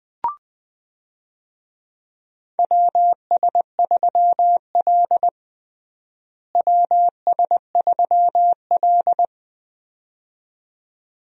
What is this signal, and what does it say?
Morse code, heard as a keyed tone, about 20 wpm, 700 Hz. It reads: WS3L WS3L
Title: USA Call Signs - 4 Characters - Repeated 2x - 20wpm